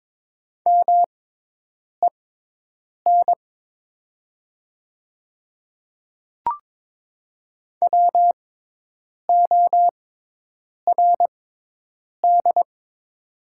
Morse code, heard as a keyed tone, about 22 wpm, spaced as for 8 wpm, 700 Hz. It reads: MEN WORD
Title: 6x Character Spacing - Top 200 Words - 22wpm